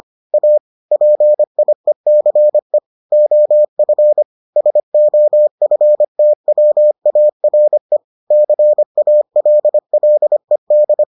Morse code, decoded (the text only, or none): A PIECE OF SOFTWARE CALLED